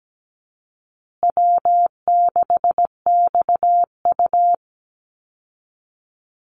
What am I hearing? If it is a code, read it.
Morse code, heard as a keyed tone, about 17 wpm, 700 Hz. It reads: W6XU
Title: Call Signs Repeated 3x - 17wpm